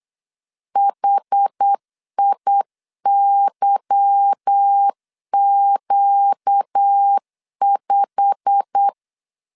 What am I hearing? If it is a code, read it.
Morse code, heard as a keyed tone, about 8 wpm, 800 Hz. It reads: HIYQ5
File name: LEZIONI / 8 YJ45 / 8 YJ45 8wpm 3214